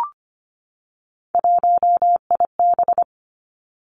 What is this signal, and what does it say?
Morse code, heard as a keyed tone, about 25 wpm, 700 Hz. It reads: 1IB
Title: Three Challenging Letter-Number Combinations 25wpm